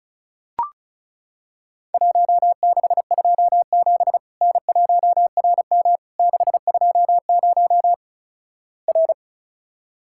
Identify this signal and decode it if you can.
Morse code, heard as a keyed tone, about 36 wpm, 700 Hz. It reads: 1627 N1RM 620 R
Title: Worked All Europe DX Contest - QTC Exchange - 35wpm